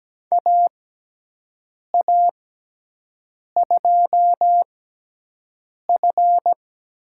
Morse code, heard as a keyed tone, about 17 wpm, spaced as for 6 wpm, 700 Hz. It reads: AA2F
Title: USA Call Signs - 5 Characters - 6x Character Spacing - 17wpm